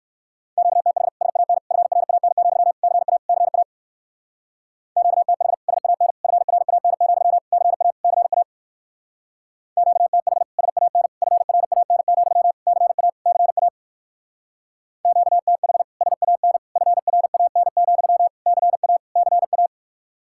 Morse code, heard as a keyed tone, about 58 wpm, 700 Hz. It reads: QTH SAN FRAN, CA CA QTH SAN FRAN, CA CA QTH SAN FRAN, CA CA QTH SAN FRAN, CA CA